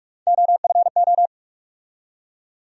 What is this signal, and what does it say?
Morse code, heard as a keyed tone, about 46 wpm, 700 Hz. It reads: OFO
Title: USA Call Sign Postfixes - 3 Letters 45wpm